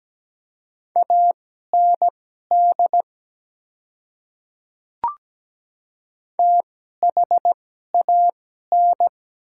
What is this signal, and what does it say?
Morse code, heard as a keyed tone, about 17 wpm, spaced as for 12 wpm, 690 Hz. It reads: AND THAN
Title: 2x Character Spacing - Top 100 Words - 17wpm